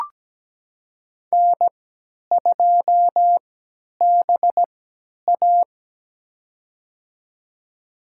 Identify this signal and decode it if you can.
Morse code, heard as a keyed tone, about 17 wpm, spaced as for 10 wpm, 700 Hz. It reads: N2BA
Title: USA Call Signs - 4 Characters - 3x Character Spacing - 17wpm